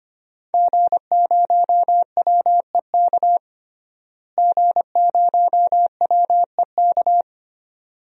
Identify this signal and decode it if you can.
Morse code, heard as a keyed tone, about 25 wpm, 700 Hz. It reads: G0WEK G0WEK
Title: DXCC-100 Call Signs - 5 Characters - Repeated 2x - 25wpm